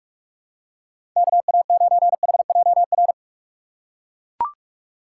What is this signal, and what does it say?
Morse code, heard as a keyed tone, about 44 wpm, 700 Hz. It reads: KA9HJR
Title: USA Call Signs - 6 Characters - 4x Character Spacing - 45wpm